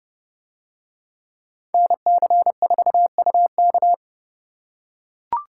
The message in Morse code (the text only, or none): NC4UK